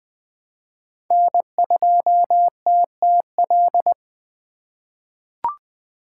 Morse code, heard as a keyed tone, about 20 wpm, 700 Hz. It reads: N2TTL
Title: USA Call Signs - 5 Characters - 2x Character Spacing - 20wpm